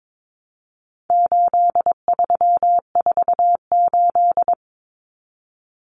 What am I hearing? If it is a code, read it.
Morse code, heard as a keyed tone, about 22 wpm, 700 Hz. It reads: 8348
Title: Common Cut Numbers - 4-Digit 22wpm